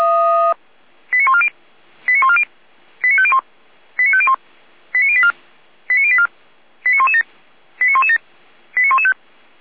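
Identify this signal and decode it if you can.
Five-tone selective call (ZVEI), heard as quick runs of five tones, tones of about 80 ms. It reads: 89140 89140 89501 89501 89084 89084 89117 89117 89115
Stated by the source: ZVEI five-tone call received by SDR.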